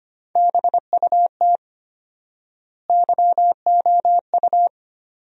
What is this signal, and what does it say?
Morse code, heard as a keyed tone, about 25 wpm, 700 Hz. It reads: BUT YOU
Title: 4x Word-Spacing - Sentences from Top 200 Words - 25wpm